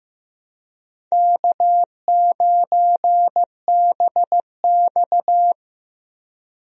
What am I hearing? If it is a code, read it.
Morse code, heard as a keyed tone, about 15 wpm, 700 Hz. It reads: K9BX
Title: USA Call Signs - 4 Characters - Repeated 2x - 15wpm